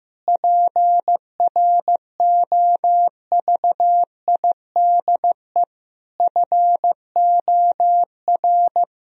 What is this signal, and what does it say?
Morse code, heard as a keyed tone, about 15 wpm, 700 Hz. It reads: PROVIDE FOR
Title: Sentences from Top 1000 Words - Review Part 3 - 15wpm